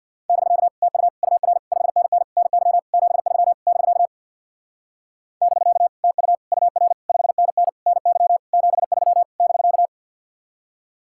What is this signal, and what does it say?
Morse code, heard as a keyed tone, about 64 wpm, 700 Hz. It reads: <BK> TU UR 5NN NY 73 <BK> <BK> TU UR 5NN NY 73 <BK>